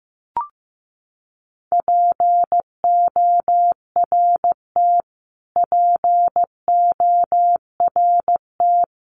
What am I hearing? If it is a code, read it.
Morse code, heard as a keyed tone, about 15 wpm, 700 Hz. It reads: PORT PORT
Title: Top 500 Words Repeated 15wpm